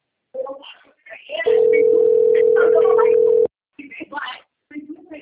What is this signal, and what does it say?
Ringback tone (North America), 440 + 480 Hz, with speech.
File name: out- - - - - .15231